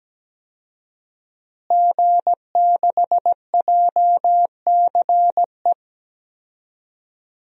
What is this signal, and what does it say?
Morse code, heard as a keyed tone, about 17 wpm, 700 Hz. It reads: G6JCE
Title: DXCC-100 Call Signs - 5 Characters - Repeated 2x - 17wpm